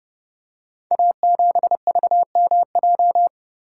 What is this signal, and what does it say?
Morse code, heard as a keyed tone, about 30 wpm, 700 Hz. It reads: A7VMJ
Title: Code Groups - 5 Letter:Numbers - 30wpm